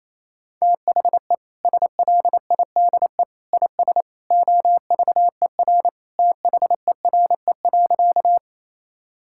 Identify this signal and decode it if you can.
Morse code, heard as a keyed tone, about 28 wpm, 700 Hz. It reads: THE SLIDE IS OVER THERE.
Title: Sentences from Top 1600 Words 28wpm